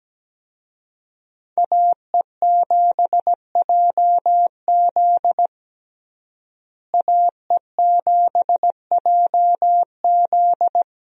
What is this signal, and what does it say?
Morse code, heard as a keyed tone, about 17 wpm, 700 Hz. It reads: AE7JZ AE7JZ
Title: USA Call Signs - 5 Characters - Repeated 3x - 17wpm